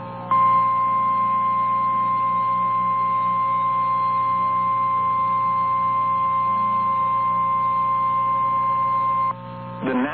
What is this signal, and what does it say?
Emergency Alert System (SAME) tones (the 1050 Hz attention tone), with noise. It is heard from 0.3 to 9.4 s.
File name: eas / KNCYKBIE / KNCYKBIE